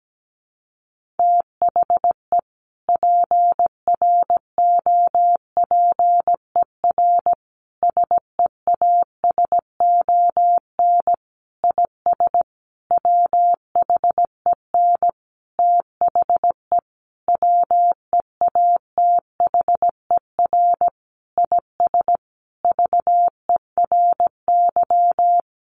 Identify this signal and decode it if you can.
Morse code, heard as a keyed tone, about 17 wpm, 700 Hz. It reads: THE PROPER SEASON IS WHEN THE WEATHER IS VERY
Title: Sentences from Top 1400 Words 17wpm